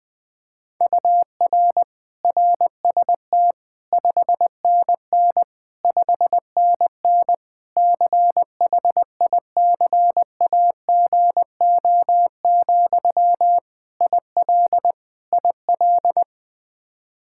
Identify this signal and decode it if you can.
Morse code, heard as a keyed tone, about 20 wpm, 700 Hz. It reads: UR RST 5NN 5NN CHICAGO, IL IL
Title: K1USN SST QSO Practice 20wpm